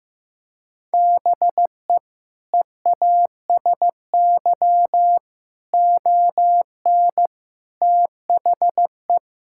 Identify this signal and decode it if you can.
Morse code, heard as a keyed tone, about 15 wpm, 700 Hz. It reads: BE EASY ON THE
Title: Sentences from Top 1600 Words 15wpm